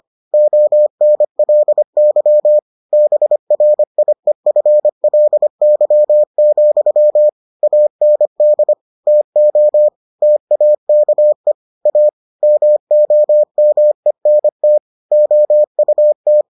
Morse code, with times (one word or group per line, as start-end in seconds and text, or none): 0.3-2.6 ONLY
2.9-7.3 BRIEFLY,
7.6-8.7 AND
9.1-9.9 TO
10.2-11.5 TAKE
11.9-12.1 A
12.4-14.8 MOMENT
15.1-16.4 OUT